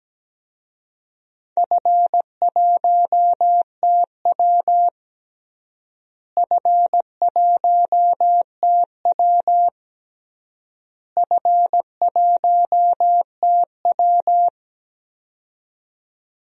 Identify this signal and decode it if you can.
Morse code, heard as a keyed tone, about 17 wpm, 700 Hz. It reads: F1TW F1TW F1TW
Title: DXCC-100 Call Signs - 4 Characters - Repeated 3x - 17wpm